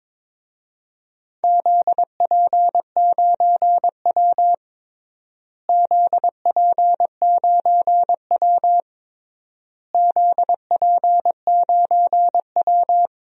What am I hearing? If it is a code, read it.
Morse code, heard as a keyed tone, about 22 wpm, 700 Hz. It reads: ZP9W ZP9W ZP9W
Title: DXCC-100 Call Signs - 4 Characters - Repeated 3x - 22wpm